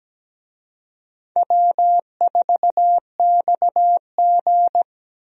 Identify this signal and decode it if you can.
Morse code, heard as a keyed tone, about 17 wpm, 700 Hz. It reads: W4XG
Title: USA Call Signs - 4 Characters - Repeated 2x - 17wpm